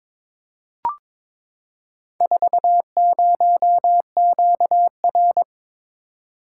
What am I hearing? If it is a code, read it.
Morse code, heard as a keyed tone, about 22 wpm, 700 Hz. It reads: 40QR
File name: practice / 22wpm / Four Letter-Number Combinations - 22wpm